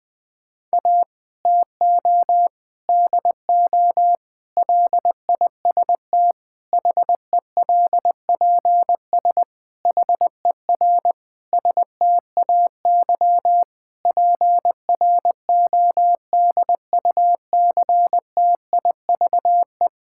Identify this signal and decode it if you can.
Morse code, heard as a keyed tone, about 20 wpm, 700 Hz. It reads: A TO DO LIST HELPS HER STAY PRODUCTIVE
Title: Difficult Sentences with Hints - 20wpm